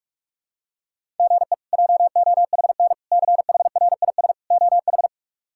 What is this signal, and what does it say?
Morse code, heard as a keyed tone, about 46 wpm, 700 Hz. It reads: GE JOHN CHRIS OH
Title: K1USN SST QSO Practice - 45wpm